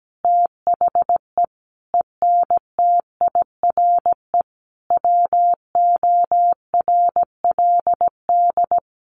Morse code, heard as a keyed tone, about 17 wpm, 700 Hz. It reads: THE ENTIRE WORLD
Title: A Sound Of Thunder 17wpm